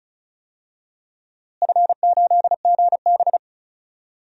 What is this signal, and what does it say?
Morse code, heard as a keyed tone, about 36 wpm, 700 Hz. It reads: F8GB
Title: DXCC-100 Call Signs - 4 Characters - Repeated 3x - 35wpm